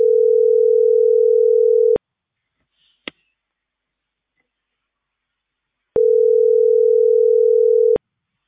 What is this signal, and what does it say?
Ringback tone (North America), 440 + 480 Hz.